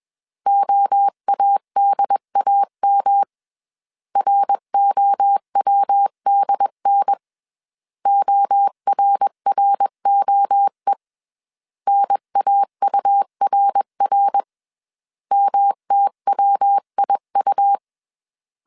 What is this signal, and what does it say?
Morse code, heard as a keyed tone, about 21 wpm, 800 Hz. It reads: OADAM ROWDN ORROE NAURR MTWIU
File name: CW / DET / 1 3 1500C S B 20wpm 1754